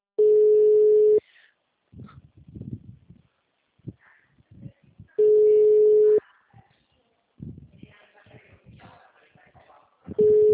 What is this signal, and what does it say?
Ringback tone (Europe, 425 Hz).
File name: out- -2022- - - .23048